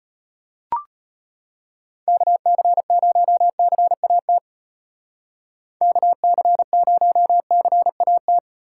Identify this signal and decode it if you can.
Morse code, heard as a keyed tone, about 34 wpm, 700 Hz. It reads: KC0CAT KC0CAT